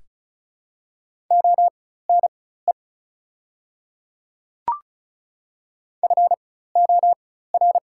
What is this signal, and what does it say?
Morse code, heard as a keyed tone, about 34 wpm, spaced as for 16 wpm, 700 Hz. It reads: ONE FOR
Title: 4x Character Spacing - Top 100 Words - 35wpm